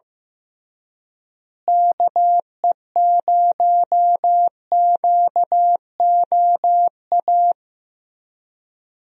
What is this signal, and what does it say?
Morse code, heard as a keyed tone, about 15 wpm, 700 Hz. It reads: KE0QOA
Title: USA Call Signs - 6 Characters - Repeated 3x - 15wpm